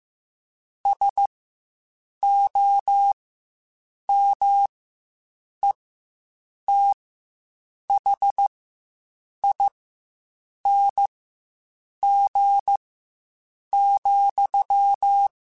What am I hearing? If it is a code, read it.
Morse code, heard as a keyed tone, about 15 wpm, spaced as for 7 wpm, 800 Hz. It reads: SOMETHING,
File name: Quotes Spaced / Quotes Spaced-mp3s / StandFor sp @10